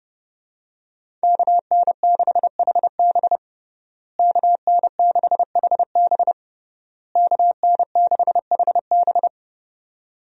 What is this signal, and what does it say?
Morse code, heard as a keyed tone, about 30 wpm, 700 Hz. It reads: KN6HB KN6HB KN6HB
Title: USA Call Signs - 5 Characters - Repeated 3x - 30wpm